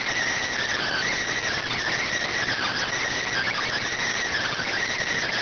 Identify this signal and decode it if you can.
Emergency Alert System (SAME) tones, heard as harsh buzzing bursts. It is heard from 0.8 to 4.8 s.